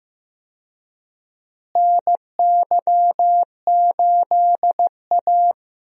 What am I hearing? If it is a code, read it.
Morse code, heard as a keyed tone, about 15 wpm, 700 Hz. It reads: NY8A
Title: USA Call Signs - 4 Characters - Repeated 3x - 15wpm